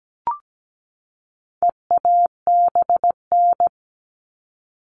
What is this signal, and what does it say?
Morse code, heard as a keyed tone, about 17 wpm, 700 Hz. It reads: EABN
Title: All Cut Numbers - 4-Digit 17wpm